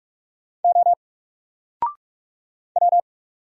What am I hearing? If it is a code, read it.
Morse code, heard as a keyed tone, about 44 wpm, 700 Hz. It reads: O W
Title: Single Letter-Number Alt 45wpm